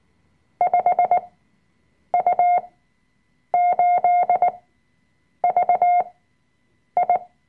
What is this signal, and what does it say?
Morse code, heard as a keyed tone, about 19 wpm, spaced as for 7 wpm, 710 Hz. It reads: 5U8VI
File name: Workshop Dateien / CW-Online-Kurs / Lektion 005